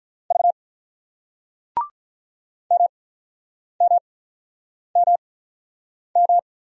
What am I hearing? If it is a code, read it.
Morse code, heard as a keyed tone, about 48 wpm, spaced as for 9 wpm, 700 Hz. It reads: V MMMM